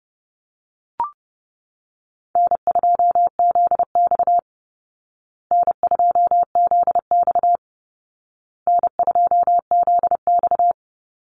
Morse code, heard as a keyed tone, about 28 wpm, 700 Hz. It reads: N2ZX N2ZX N2ZX
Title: Easy Call Signs Repeated 3x - 30wpm